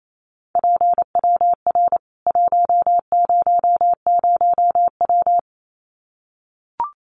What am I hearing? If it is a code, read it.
Morse code, heard as a keyed tone, about 28 wpm, 700 Hz. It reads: PWR 100W
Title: Common QSO Phrases 28wpm